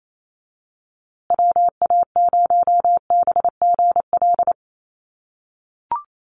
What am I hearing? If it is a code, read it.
Morse code, heard as a keyed tone, about 28 wpm, 700 Hz. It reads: WA0BGL
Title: USA Call Signs Repeated 3x 28wpm